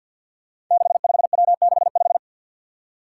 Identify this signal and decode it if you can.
Morse code, heard as a keyed tone, about 52 wpm, 700 Hz. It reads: 65W65
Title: Five Challenging Letter-Number Combinations 50wpm